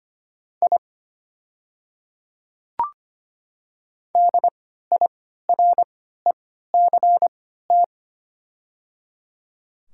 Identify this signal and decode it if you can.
Morse code, heard as a keyed tone, about 25 wpm, spaced as for 14 wpm, 700 Hz. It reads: I DIRECT
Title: 3x Character Spacing - Top 500 Words - 25wpm